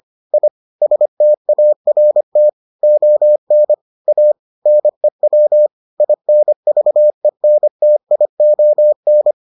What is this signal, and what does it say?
Morse code, heard as a keyed tone, about 25 wpm, 600 Hz. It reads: I START ON A NEW INVENTION